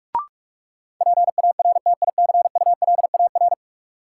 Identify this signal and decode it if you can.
Morse code, heard as a keyed tone, about 46 wpm, 700 Hz. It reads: PARTICULAR
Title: Sentences from Top 600 Words - New Word Drill - 45wpm